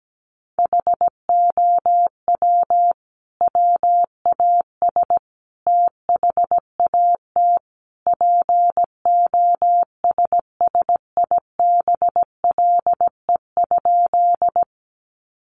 Morse code, heard as a keyed tone, about 17 wpm, 700 Hz. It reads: HOW WAS THAT POSSIBLE?
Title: Sentences from Top 500 Words 17wpm